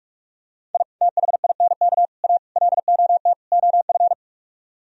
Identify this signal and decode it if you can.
Morse code, heard as a keyed tone, about 49 wpm, 700 Hz. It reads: I THINK A LOT OF